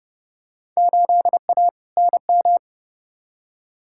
Morse code, heard as a keyed tone, about 30 wpm, 700 Hz. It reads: 8A NM
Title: Field Day Exchanges Encore - 30wpm